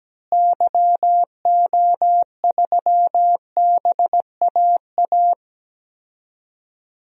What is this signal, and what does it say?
Morse code, heard as a keyed tone, about 17 wpm, 700 Hz. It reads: YO3BAA
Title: DXCC-100 Call Signs - 6 Characters - Repeated 3x - 17wpm